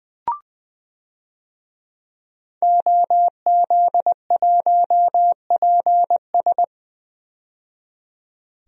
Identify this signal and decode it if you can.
Morse code, heard as a keyed tone, about 20 wpm, 700 Hz. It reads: OZ1PS
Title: DXCC-100 Call Signs - 5 Characters - 20wpm